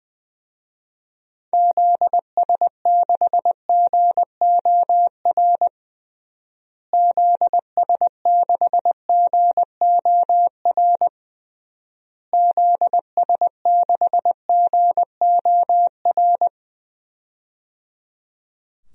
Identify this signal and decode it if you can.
Morse code, heard as a keyed tone, about 20 wpm, 700 Hz. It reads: ZS6GOR ZS6GOR ZS6GOR
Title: DXCC-100 Call Signs - 6 Characters - Repeated 3x - 20wpm